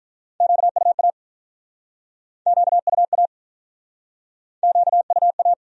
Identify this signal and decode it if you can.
Morse code, heard as a keyed tone, about 50 wpm, 700 Hz. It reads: QUA QUA QUA